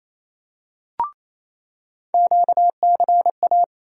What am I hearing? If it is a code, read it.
Morse code, heard as a keyed tone, about 28 wpm, 700 Hz. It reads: QCA
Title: USA Call Sign Postfixes - 3 Letters 28wpm